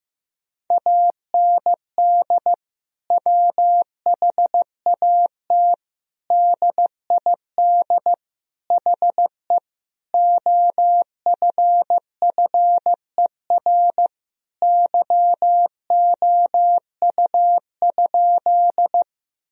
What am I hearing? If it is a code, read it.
Morse code, heard as a keyed tone, about 15 wpm, 700 Hz. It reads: AND WHAT DID HE OFFER YOU?
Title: Sentences from Top 1000 Words - Review Part 4 - 15wpm